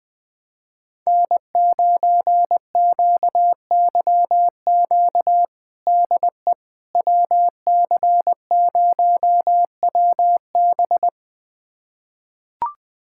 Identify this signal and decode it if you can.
Morse code, heard as a keyed tone, about 20 wpm, 700 Hz. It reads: N9QYQ DE WC0WB
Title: USA Callsign DE Callsign - 5 Characters - 20wpm